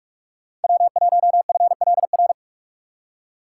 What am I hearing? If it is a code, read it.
Morse code, heard as a keyed tone, about 46 wpm, 700 Hz. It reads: W1FLR
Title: USA Call Signs - 5 Characters - 4x Character Spacing - 45wpm